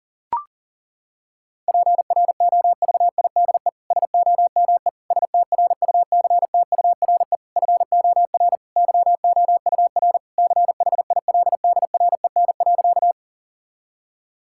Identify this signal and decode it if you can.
Morse code, heard as a keyed tone, about 40 wpm, 700 Hz. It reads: PROVIDE SOME STRUCTURE FOR YOUR CHILDREN.
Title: Sentences from Top 1000 Words - Review Part 6 - 40wpm